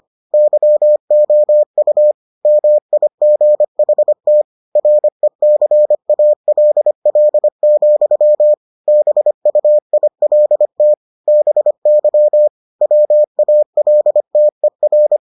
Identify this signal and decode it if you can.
Morse code, heard as a keyed tone, about 25 wpm, 600 Hz. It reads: YOU MIGHT RECALL, BUILT BY WALTER